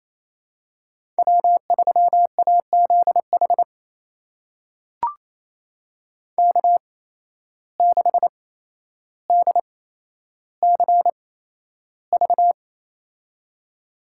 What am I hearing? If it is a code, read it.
Morse code, heard as a keyed tone, about 28 wpm, spaced as for 12 wpm, 700 Hz. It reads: W3AZH K6DCV
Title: USA Call Signs - 5 Characters - 8x Character Spacing - 28wpm